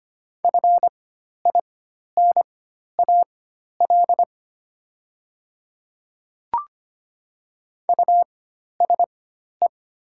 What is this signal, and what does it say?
Morse code, heard as a keyed tone, about 25 wpm, spaced as for 12 wpm, 700 Hz. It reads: FINAL USE
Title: 4x Character Spacing - Top 500 Words - 25wpm